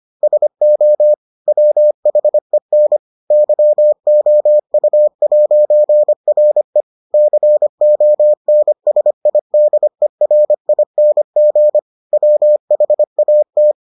Morse code, heard as a keyed tone, about 25 wpm, 600 Hz. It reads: SO WHEN YOU'RE CONSIDERING WHAT